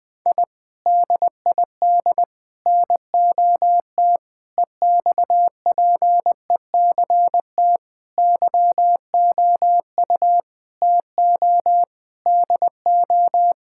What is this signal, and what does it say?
Morse code, heard as a keyed tone, about 20 wpm, 700 Hz. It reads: I DID NOT EXPECT YOU TO DO
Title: Sentences with 3-Word Phrases - 20wpm